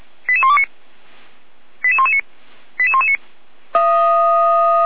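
Five-tone selective call (ZVEI), heard as quick runs of five tones, tones of about 70 ms. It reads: 88129 88119 88119
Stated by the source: ZVEI five-tone call received by SDR, from feuerwehr-orlamuende.org fire brigade audio archive.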